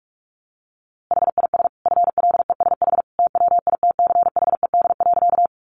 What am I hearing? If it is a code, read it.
Morse code, heard as a keyed tone, about 44 wpm, 700 Hz. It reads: HIS FLESH TWITCHED.
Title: A Sound Of Thunder 45wpm